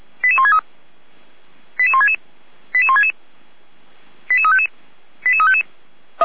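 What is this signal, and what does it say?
Five-tone selective call (ZVEI), heard as quick runs of five tones, tones of about 80 ms. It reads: 88163 88166 88166 88355 88355
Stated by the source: ZVEI five-tone call 88191 received by SDR.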